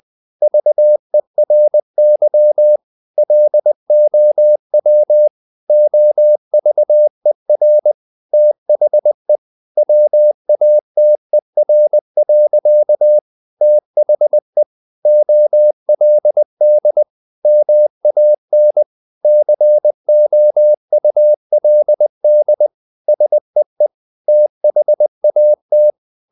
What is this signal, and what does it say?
Morse code, heard as a keyed tone, about 20 wpm, 600 Hz. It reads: VERY LOW OVER THE WATER. THE OLD MAN COULD SEE THAT
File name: the old man and the sea / the old man and the sea part 1 20wpm